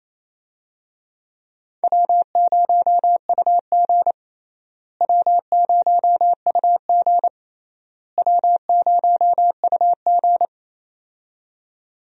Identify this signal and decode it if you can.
Morse code, heard as a keyed tone, about 28 wpm, 700 Hz. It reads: W0UG W0UG W0UG
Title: USA Call Signs - 4 Characters - Repeated 3x - 28wpm